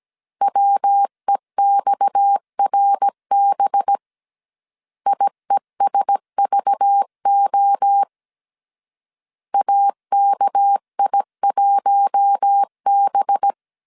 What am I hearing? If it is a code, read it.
Morse code, heard as a keyed tone, about 17 wpm, 800 Hz. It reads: WEXRB IESVO AKI1B
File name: 16wpm